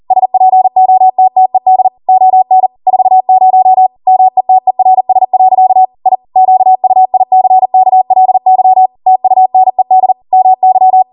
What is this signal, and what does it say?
Morse code, heard as a keyed tone, about 40 wpm, 750 Hz. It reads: SPOTTED ON 40 METERS. I QUICKLY TUNED MY